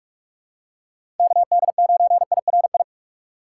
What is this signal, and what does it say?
Morse code, heard as a keyed tone, about 47 wpm, 700 Hz. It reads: KD9IRI
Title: USA Call Signs - 6x Character Spacing - 45wpm